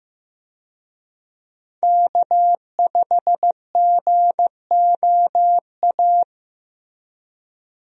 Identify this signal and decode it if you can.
Morse code, heard as a keyed tone, about 15 wpm, 700 Hz. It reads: K5GOA
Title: USA Call Signs - 5 Characters - Repeated 3x - 15wpm